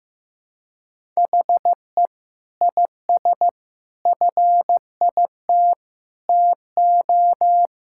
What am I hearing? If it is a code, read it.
Morse code, heard as a keyed tone, about 16 wpm, 700 Hz. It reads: HE IS FIT TO
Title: Sentences from Top 1000 Words - Review Part 5 - 15wpm